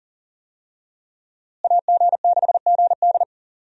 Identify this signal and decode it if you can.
Morse code, heard as a keyed tone, about 40 wpm, 700 Hz. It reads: AG6GD
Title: USA Call Signs - 5 Characters - Repeated 2x - 40wpm